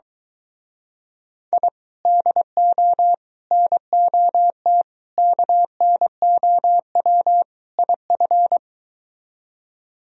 Morse code, heard as a keyed tone, about 24 wpm, 700 Hz. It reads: I DO NOT KNOW IF